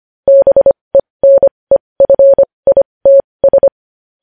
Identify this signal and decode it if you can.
Morse code, heard as a keyed tone, about 25 wpm, spaced as for 20 wpm, 550 Hz. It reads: BENEFITS
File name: podcast / News headlines-1x-25x20